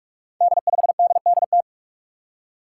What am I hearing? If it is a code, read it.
Morse code, heard as a keyed tone, about 46 wpm, 700 Hz. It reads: D5DDT
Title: Five Challenging Letter-Number Combinations 45wpm